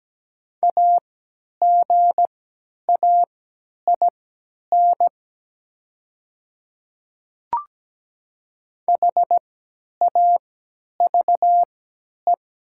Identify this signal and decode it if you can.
Morse code, heard as a keyed tone, about 17 wpm, spaced as for 10 wpm, 700 Hz. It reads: AGAIN HAVE
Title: 3x Character Spacing - Top 300 Words - 17wpm